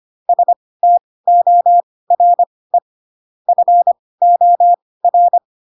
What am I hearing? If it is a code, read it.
Morse code, heard as a keyed tone, about 25 wpm, spaced as for 18 wpm, 700 Hz. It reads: STORE FOR
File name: Pinocchio / Pinocchio ch7 12 25 ik1hge